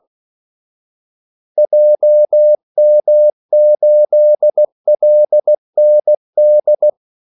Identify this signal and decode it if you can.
Morse code, heard as a keyed tone, about 16 wpm, 600 Hz. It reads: JM8LND
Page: Kaldesignaler 16 wpm